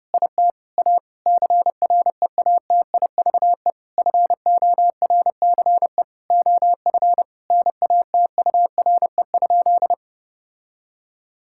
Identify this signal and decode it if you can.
Morse code, heard as a keyed tone, about 30 wpm, 700 Hz. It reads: IT A CREATIVE FORCE OF NATURE?
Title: Sentences from Top 1000 Words - Review Part 5 - 30wpm